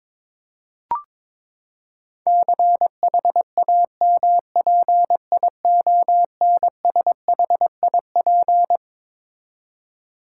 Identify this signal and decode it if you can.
Morse code, heard as a keyed tone, about 22 wpm, 700 Hz. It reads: CHAMPIONSHIP
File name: Twelve Letter Words - 22wpm